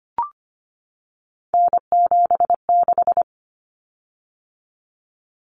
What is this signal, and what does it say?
Morse code, heard as a keyed tone, about 25 wpm, 700 Hz. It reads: N76
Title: Common Cut Numbers - 3-Digit 25wpm